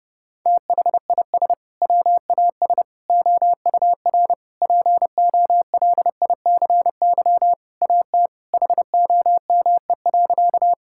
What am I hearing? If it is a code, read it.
Morse code, heard as a keyed tone, about 30 wpm, 700 Hz. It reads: THIS WAS OUR POLICY AT HOME.
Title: Sentences from Top 1000 Words - Review Part 5 - 30wpm